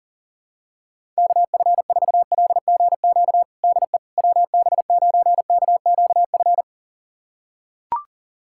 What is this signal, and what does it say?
Morse code, heard as a keyed tone, about 40 wpm, 700 Hz. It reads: KF4LGQ DE WB9KQF
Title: USA Callsign DE Callsign - 6 Characters - 40wpm